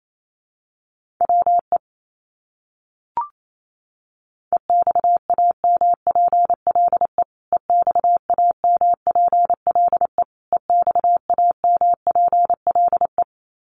Morse code, heard as a keyed tone, about 28 wpm, 700 Hz. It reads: WE EXAMPLE EXAMPLE EXAMPLE
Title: Top 300 Words Repeated 28wpm